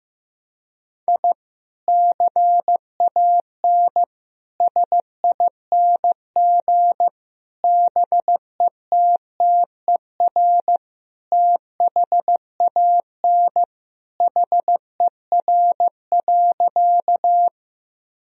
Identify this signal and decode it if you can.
Morse code, heard as a keyed tone, about 16 wpm, 700 Hz. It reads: I CAN SING BETTER THAN HER.
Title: Sentences from Top 1000 Words - Review Part 2 - 15wpm